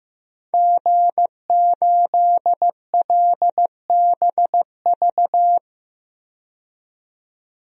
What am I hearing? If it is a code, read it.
Morse code, heard as a keyed tone, about 16 wpm, 700 Hz. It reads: G8LBV
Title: DXCC-100 Call Signs - 5 Characters - 15wpm